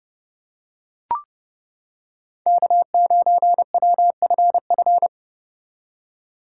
Morse code, heard as a keyed tone, about 30 wpm, 700 Hz. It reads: K9WFF